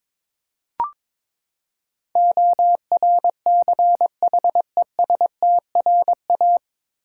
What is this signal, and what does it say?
Morse code, heard as a keyed tone, about 22 wpm, 700 Hz. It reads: ORCHESTRA
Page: Morse Code Ninja practice 22wpm